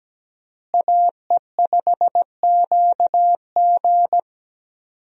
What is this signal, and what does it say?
Morse code, heard as a keyed tone, about 17 wpm, 700 Hz. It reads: AE5QG
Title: USA Call Signs - 5 Characters - Repeated 3x - 17wpm